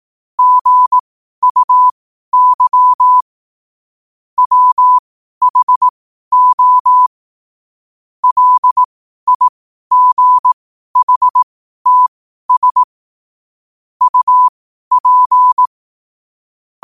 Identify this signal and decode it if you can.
Morse code, heard as a keyed tone, about 18 wpm, spaced as for 12 wpm, 1000 Hz. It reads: GUY WHO LIGHTS UP
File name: qotdfiles / 12wpm